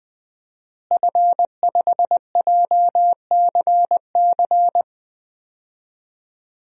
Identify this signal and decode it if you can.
Morse code, heard as a keyed tone, about 20 wpm, 700 Hz. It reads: F5JCC
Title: DXCC-100 Call Signs - 5 Characters - Repeated 3x - 20wpm